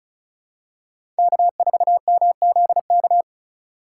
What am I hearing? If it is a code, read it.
Morse code, heard as a keyed tone, about 36 wpm, 700 Hz. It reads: K4MZK
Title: USA Call Signs - 5 Characters - 2x Character Spacing - 35wpm